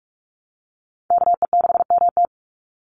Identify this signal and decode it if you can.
Morse code, heard as a keyed tone, about 44 wpm, 700 Hz. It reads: KE6MT
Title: NA SOTA Chaser Callsigns 45wpm